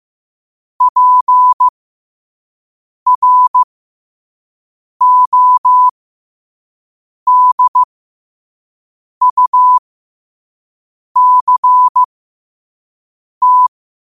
Morse code, heard as a keyed tone, about 15 wpm, spaced as for 5 wpm, 1000 Hz. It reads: PRODUCT
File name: qotdfiles / 5wpm